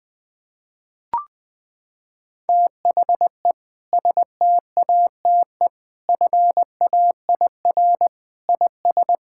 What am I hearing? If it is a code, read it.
Morse code, heard as a keyed tone, about 20 wpm, 700 Hz. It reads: THE STATE FAIR IS
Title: Sentences with 2-Word Phrases - 20wpm